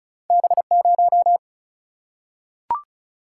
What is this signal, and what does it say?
Morse code, heard as a keyed tone, about 36 wpm, 700 Hz. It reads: B0
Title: Single-Letter Single-Number 35wpm